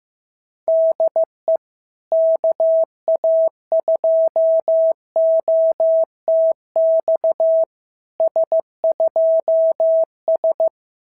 Morse code, heard as a keyed tone, about 15 wpm, 650 Hz. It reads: DE KA2OTX S2S
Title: SOTA QSO Practice 15wpm